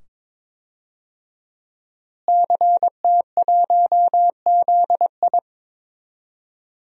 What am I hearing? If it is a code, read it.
Morse code, heard as a keyed tone, about 22 wpm, 700 Hz. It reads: CT1ZI
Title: DXCC-100 Call Signs - 5 Characters - Repeated 2x - 22wpm